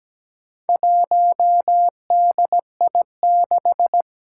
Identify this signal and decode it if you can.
Morse code, heard as a keyed tone, about 17 wpm, 700 Hz. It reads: 1DI6
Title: Four Challenging Letter-Number Combinations 17wpm